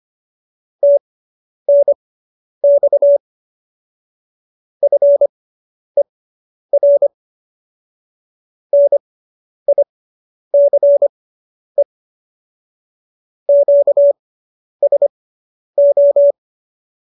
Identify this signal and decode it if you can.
Morse code, heard as a keyed tone, about 25 wpm, spaced as for 10 wpm, 570 Hz. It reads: TNX FER NICE QSO